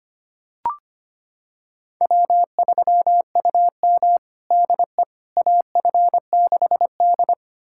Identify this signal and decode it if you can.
Morse code, heard as a keyed tone, about 25 wpm, 700 Hz. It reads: W3UM DE AF6D
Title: USA Callsign DE Callsign - 4 Characters - 25wpm